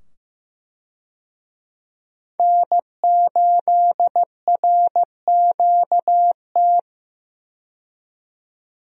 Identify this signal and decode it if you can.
Morse code, heard as a keyed tone, about 15 wpm, 700 Hz. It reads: N8RQT
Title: USA Call Signs - 5 Characters - 15wpm